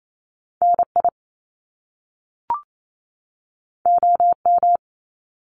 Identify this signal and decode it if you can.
Morse code, heard as a keyed tone, about 28 wpm, 700 Hz. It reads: NI OM
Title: Top Two Letter Combinations in Words 28wpm